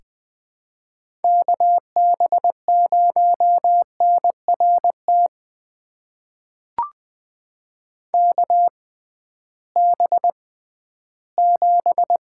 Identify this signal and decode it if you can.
Morse code, heard as a keyed tone, about 20 wpm, spaced as for 13 wpm, 700 Hz. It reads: KB0NRT KB7
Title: USA Call Signs - 6 Characters - 6x Character Spacing - 20wpm